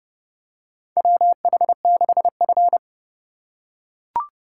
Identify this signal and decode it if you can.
Morse code, heard as a keyed tone, about 32 wpm, 700 Hz. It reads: WH6F
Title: USA Call Signs - 4 Characters - 3x Character Spacing - 30wpm